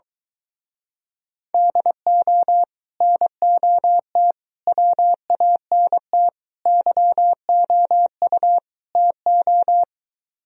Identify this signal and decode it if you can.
Morse code, heard as a keyed tone, about 24 wpm, 700 Hz. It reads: DO NOT WANT YOU TO